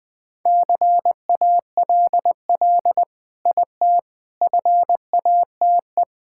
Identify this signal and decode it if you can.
Morse code, heard as a keyed tone, about 20 wpm, 700 Hz. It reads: CALL IT FATE
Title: Sentences with 7-Word Phrases - Spoken Beginning - 20wpm